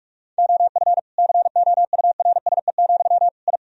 Morse code, heard as a keyed tone, about 41 wpm, 700 Hz. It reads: OF COURSE, I
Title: Sentences from Top 1000 Words - Review Part 4 - 45wpm